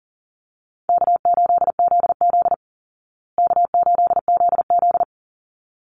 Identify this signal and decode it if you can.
Morse code, heard as a keyed tone, about 40 wpm, 700 Hz. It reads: K8ZZ K8ZZ
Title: Call Signs Repeated 3x - 40wpm